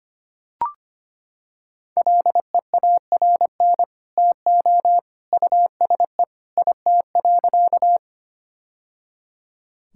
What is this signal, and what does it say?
Morse code, heard as a keyed tone, about 25 wpm, 700 Hz. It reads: LEARN TO USE IT.
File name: practice / 25wpm / Sentences with 2-Word Phrases - 25wpm